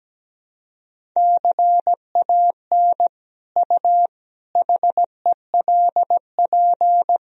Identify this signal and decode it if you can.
Morse code, heard as a keyed tone, about 17 wpm, 700 Hz. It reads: CAN U HELP
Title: Sentences from Top 200 Words and Abbreviations 17wpm